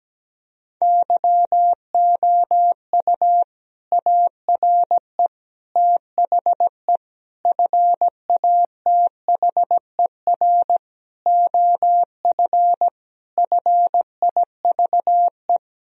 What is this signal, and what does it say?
Morse code, heard as a keyed tone, about 17 wpm, 700 Hz. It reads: YOU ARE THE FATHER OF FIVE
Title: Sentences from Top 1000 Words - Review Part 2 - 17wpm